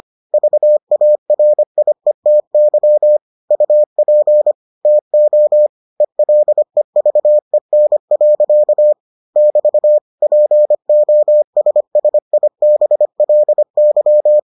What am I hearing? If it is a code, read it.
Morse code, heard as a keyed tone, about 25 wpm, 600 Hz. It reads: VARIETY UP TO ELEVEN. <BT> POSSIBLY